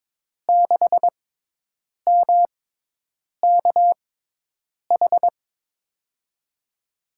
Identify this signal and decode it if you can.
Morse code, heard as a keyed tone, about 22 wpm, spaced as for 8 wpm, 700 Hz. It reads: 6MKH
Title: USA Call Signs - 6 Characters - 6x Character Spacing - 22wpm